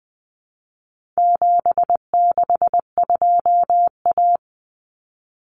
Morse code, heard as a keyed tone, about 20 wpm, 700 Hz. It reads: 762A
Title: Common Cut Numbers - 4-Digit 20wpm